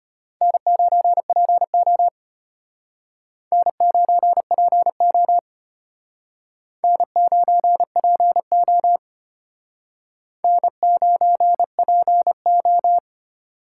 Morse code, heard as a keyed tone, about 33 wpm, 700 Hz. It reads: N9PO N9PO N9PO N9PO